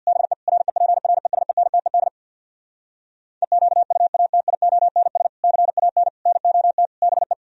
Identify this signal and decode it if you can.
Morse code, heard as a keyed tone, about 49 wpm, 700 Hz. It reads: BE REPRESENTED EQUATIONS CAN NOT BE
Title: Speed-Racing - Sets of 5 Words 50wpm